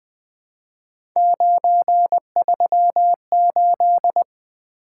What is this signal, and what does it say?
Morse code, heard as a keyed tone, about 20 wpm, 700 Hz. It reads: 938
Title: Three Letter-Number Combinations - 20wpm